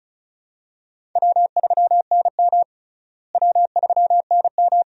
Morse code, heard as a keyed tone, about 36 wpm, 700 Hz. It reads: W3NM W3NM
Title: USA Call Signs - 4 Characters - Repeated 3x - 35wpm